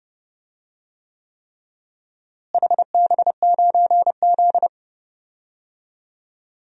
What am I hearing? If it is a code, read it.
Morse code, heard as a keyed tone, about 31 wpm, 700 Hz. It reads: HB9Z
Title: DXCC-100 Call Signs - 4 Characters - Repeated 3x - 30wpm